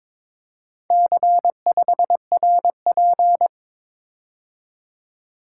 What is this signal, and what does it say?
Morse code, heard as a keyed tone, about 23 wpm, 700 Hz. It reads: C5RP
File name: practice / 22wpm / Four Letter-Number Combinations - 22wpm